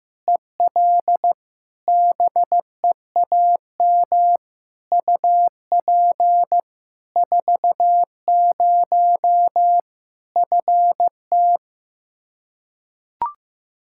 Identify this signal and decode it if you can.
Morse code, heard as a keyed tone, about 15 wpm, 700 Hz. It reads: EL BEAM UP 40 FT
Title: Practice QSO - Exchange 2 - Info - 15wpm